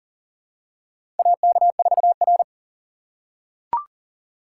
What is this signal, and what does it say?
Morse code, heard as a keyed tone, about 40 wpm, 700 Hz. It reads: AK4R
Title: USA Call Signs - 4 Characters - 2x Character Spacing - 40wpm